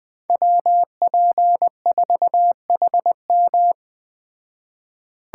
Morse code, heard as a keyed tone, about 20 wpm, 700 Hz. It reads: WP4HM
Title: Speed Racing - USA 2x2 Call Signs 17wpm